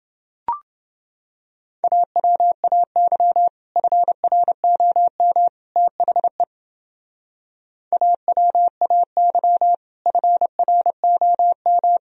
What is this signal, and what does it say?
Morse code, heard as a keyed tone, about 29 wpm, 700 Hz. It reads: AWAY FROM THE AWAY FROM